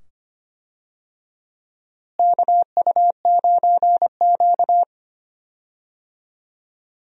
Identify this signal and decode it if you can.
Morse code, heard as a keyed tone, about 25 wpm, 700 Hz. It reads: KU9Q
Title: USA Call Signs - 4 Characters - Repeated 2x - 25wpm